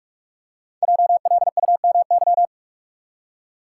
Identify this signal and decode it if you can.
Morse code, heard as a keyed tone, about 44 wpm, 700 Hz. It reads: JLUMY
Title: Code Groups - 5 Letters - 45wpm